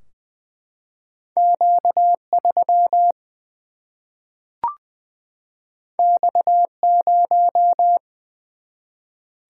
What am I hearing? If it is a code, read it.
Morse code, heard as a keyed tone, about 20 wpm, 700 Hz. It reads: Q 3 X 0
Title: Single-Letter Single-Number 20wpm